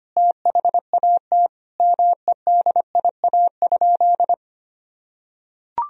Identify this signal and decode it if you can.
Morse code, heard as a keyed tone, about 25 wpm, 700 Hz. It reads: THAT MEDIA?
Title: Sentences from Top 1000 Words - Review Part 4 - 25wpm